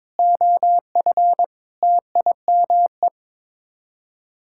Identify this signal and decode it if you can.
Morse code, heard as a keyed tone, about 22 wpm, 700 Hz. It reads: OF TIME
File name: practice / 22wpm / Sets of 7 Words - 22wpm